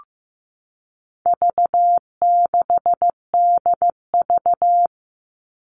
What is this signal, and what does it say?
Morse code, heard as a keyed tone, about 15 wpm, 700 Hz. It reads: V6DV
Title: All Cut Numbers - 4-Digit 15wpm